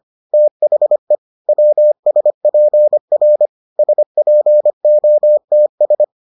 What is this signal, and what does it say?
Morse code, heard as a keyed tone, about 25 wpm, 600 Hz. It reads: THE WSPR SPOTS